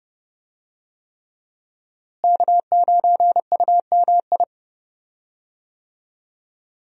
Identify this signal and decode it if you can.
Morse code, heard as a keyed tone, about 32 wpm, 700 Hz. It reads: K9UMI
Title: USA Call Signs - 5 Characters - Repeated 3x - 30wpm